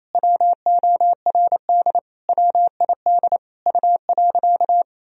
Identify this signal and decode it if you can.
Morse code, heard as a keyed tone, about 28 wpm, 700 Hz. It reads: WORD WID U.
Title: Sentences from Top 300 Words and Abbreviations 28wpm